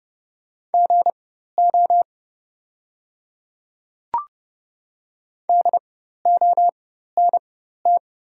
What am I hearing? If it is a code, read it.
Morse code, heard as a keyed tone, about 30 wpm, spaced as for 14 wpm, 700 Hz. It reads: GO DONT
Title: 4x Character Spacing - Top 500 Words - 30wpm